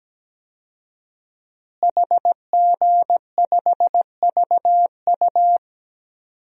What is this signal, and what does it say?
Morse code, heard as a keyed tone, about 17 wpm, 700 Hz. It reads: HG5VU
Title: DXCC-100 Call Signs - 5 Characters - Repeated 3x - 17wpm